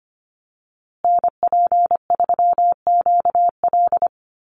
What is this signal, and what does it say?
Morse code, heard as a keyed tone, about 25 wpm, 700 Hz. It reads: NP3QL
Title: USA Call Signs Repeated 3x 25wpm